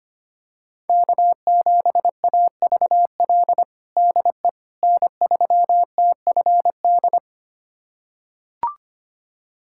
Morse code, heard as a keyed tone, about 25 wpm, 700 Hz. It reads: K7AVL DE N3TFD
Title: USA Callsign DE Callsign - 5 Characters - 25wpm